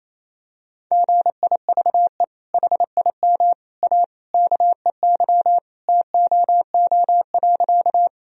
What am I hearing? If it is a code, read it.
Morse code, heard as a keyed tone, about 28 wpm, 700 Hz. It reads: GIVE HIM A KEY TOO.
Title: Sentences from Top 1000 Words - Review Part 4 - 28wpm